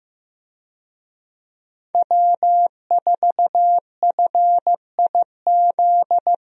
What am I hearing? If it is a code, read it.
Morse code, heard as a keyed tone, about 15 wpm, 700 Hz. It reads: W4FIZ
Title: USA Call Signs - 5 Characters - Repeated 2x - 15wpm